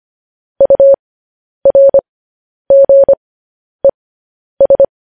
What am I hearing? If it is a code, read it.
Morse code, heard as a keyed tone, about 25 wpm, spaced as for 10 wpm, 550 Hz. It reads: URGES
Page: news-headlines-in-morse-code-at-10-wpm